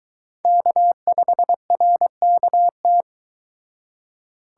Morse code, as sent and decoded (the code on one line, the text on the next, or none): -.- ..... .-. -.- -
K5RKT